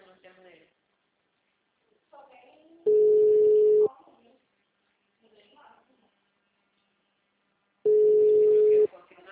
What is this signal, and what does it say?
Ringback tone (Europe, 425 Hz), with speech.